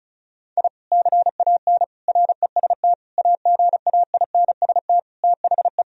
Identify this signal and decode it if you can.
Morse code, heard as a keyed tone, about 36 wpm, 700 Hz. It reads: I CAN REST AGAINST THE
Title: Sentences from Top 1000 Words - Review Part 1 - 35wpm